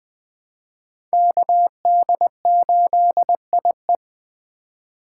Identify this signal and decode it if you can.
Morse code, heard as a keyed tone, about 20 wpm, 700 Hz. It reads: KD8IE
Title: 2023 POTA Activator Call Signs - 20wpm